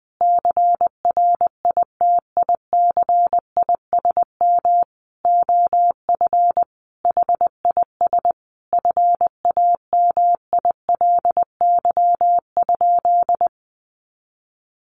Morse code, heard as a keyed tone, about 20 wpm, 700 Hz. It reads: CRITICISM OF HIS FAMILY?
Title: Sentences from Top 1300 Words 20wpm